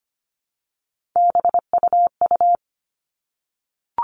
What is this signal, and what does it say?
Morse code, heard as a keyed tone, about 25 wpm, 700 Hz. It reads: BUU
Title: All Cut Numbers - 3-Digit 25wpm